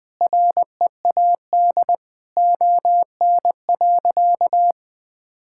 Morse code, heard as a keyed tone, about 20 wpm, 700 Hz. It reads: READ ON.
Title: Sentences with 9-Word Phrases - 20wpm